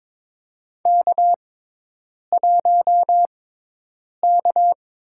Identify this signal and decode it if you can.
Morse code, heard as a keyed tone, about 22 wpm, spaced as for 8 wpm, 700 Hz. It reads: K1K
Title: USA Call Signs - 6x Character Spacing - 22wpm